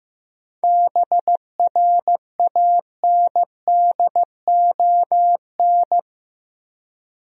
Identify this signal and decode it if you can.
Morse code, heard as a keyed tone, about 15 wpm, 700 Hz. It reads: BRANDON
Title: USA Common Names - 15wpm